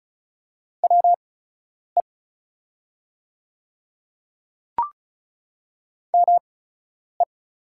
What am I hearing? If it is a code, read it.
Morse code, heard as a keyed tone, about 36 wpm, spaced as for 10 wpm, 700 Hz. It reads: WE ME
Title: 8x Character Spacing - Top 200 Words - 35wpm